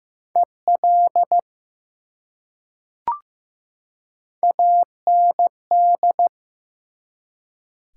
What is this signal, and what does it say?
Morse code, heard as a keyed tone, about 15 wpm, 700 Hz. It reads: EL AND
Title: Top 180 QSO Elements 15wpm